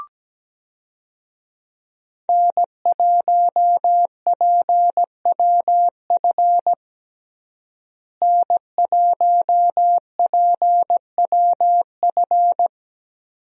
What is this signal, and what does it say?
Morse code, heard as a keyed tone, about 17 wpm, 700 Hz. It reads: N1PWF N1PWF
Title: USA Call Signs - 5 Characters - Repeated 3x - 17wpm